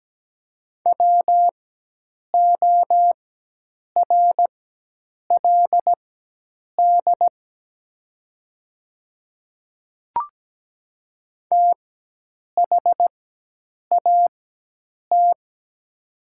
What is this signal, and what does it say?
Morse code, heard as a keyed tone, about 17 wpm, spaced as for 8 wpm, 700 Hz. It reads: WORLD THAT
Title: 4x Character Spacing - Top 200 Words - 17wpm